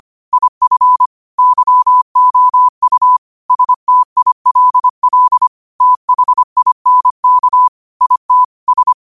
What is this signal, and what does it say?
Morse code, heard as a keyed tone, about 25 wpm, 1000 Hz. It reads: IF YOU STILL THINK IT S
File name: qotdfiles / 25wpm